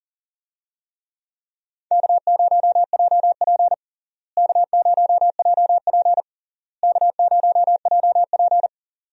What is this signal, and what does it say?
Morse code, heard as a keyed tone, about 40 wpm, 700 Hz. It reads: K0JP K0JP K0JP
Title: USA Call Signs - 4 Characters - Repeated 3x - 40wpm